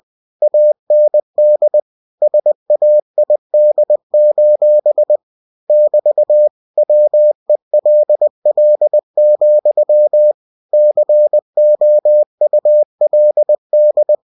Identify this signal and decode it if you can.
Morse code, heard as a keyed tone, about 20 wpm, 600 Hz. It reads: AND SAID: <BT> WELL, COULD
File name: cw-books / winnie-the-pooh / winnie-20wpm-page-0014